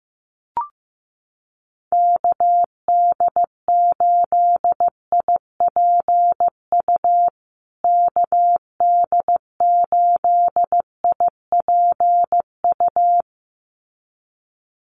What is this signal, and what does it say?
Morse code, heard as a keyed tone, about 15 wpm, 700 Hz. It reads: KD8IPU KD8IPU
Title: USA Call Signs Repeated 2x 15wpm